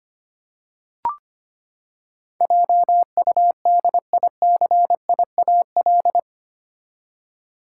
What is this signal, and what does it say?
Morse code, heard as a keyed tone, about 25 wpm, 700 Hz. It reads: JUDICIAL
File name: Eight Letter Words - 25wpm